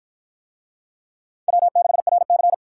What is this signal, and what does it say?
Morse code, heard as a keyed tone, about 48 wpm, 700 Hz. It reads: W6RC